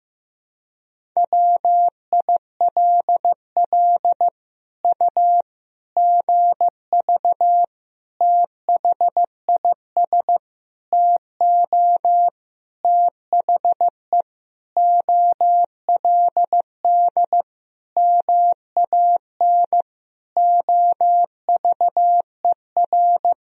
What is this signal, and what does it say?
Morse code, heard as a keyed tone, about 15 wpm, 700 Hz. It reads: WILL U GV THIS TO THE OLD MAN OVER
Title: Sentences from Top 200 Words and Abbreviations 15wpm